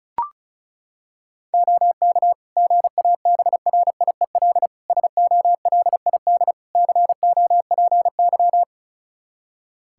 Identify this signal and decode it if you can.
Morse code, heard as a keyed tone, about 34 wpm, 700 Hz. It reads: OK GABRIEL SOLID COPY
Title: Practice QSO - Exchange 3 - More Info - 35wpm